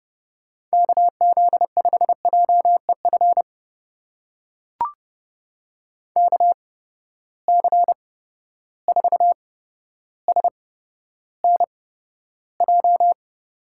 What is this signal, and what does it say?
Morse code, heard as a keyed tone, about 32 wpm, spaced as for 13 wpm, 700 Hz. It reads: KZ5JEF KC4SNJ
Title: USA Call Signs - 6 Characters - 8x Character Spacing - 30wpm